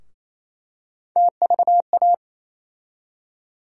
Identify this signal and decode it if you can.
Morse code, heard as a keyed tone, about 27 wpm, 700 Hz. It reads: TVA
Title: USA Call Sign Postfixes - 3 Letters 28wpm